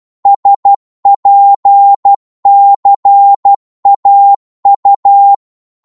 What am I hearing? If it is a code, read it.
Morse code, heard as a keyed tone, about 12 wpm, 800 Hz. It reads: SPCAU
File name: Morse / ltr-12wpm